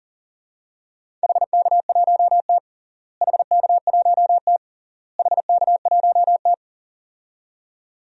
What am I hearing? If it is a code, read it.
Morse code, heard as a keyed tone, about 40 wpm, 700 Hz. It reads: HK1T HK1T HK1T
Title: DXCC-100 Call Signs - 4 Characters - Repeated 3x - 40wpm